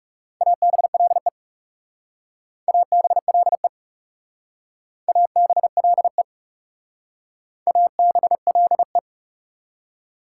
Morse code, heard as a keyed tone, about 35 wpm, 690 Hz. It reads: ABLE ABLE ABLE ABLE
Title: Speed-Racing Prefixes and Suffixes 30wpm